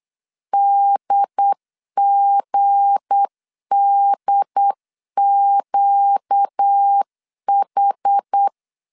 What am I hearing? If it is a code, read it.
Morse code, heard as a keyed tone, about 8 wpm, 800 Hz. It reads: DGDQH
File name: LEZIONI / 7 GF9Z / 7 G 8wpm 3118